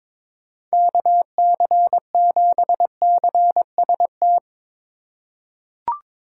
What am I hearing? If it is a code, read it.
Morse code, heard as a keyed tone, about 22 wpm, 700 Hz. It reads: KC7CST
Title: USA Call Signs - 6 Characters - 8x Character Spacing - 22wpm